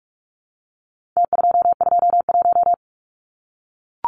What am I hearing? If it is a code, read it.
Morse code, heard as a keyed tone, about 44 wpm, 700 Hz. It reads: T221
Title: Numbers - Contest Style - 4-Digit 45wpm